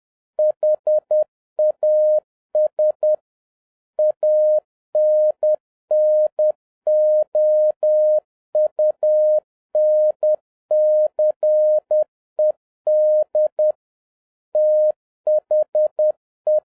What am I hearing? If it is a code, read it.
Morse code, heard as a keyed tone, about 10 wpm, 610 Hz. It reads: HAS ANNOUNCED THE